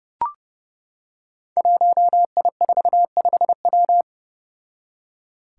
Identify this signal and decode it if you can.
Morse code, heard as a keyed tone, about 28 wpm, 700 Hz. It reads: 1I45W
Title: Five Challenging Letter-Number Combinations 30wpm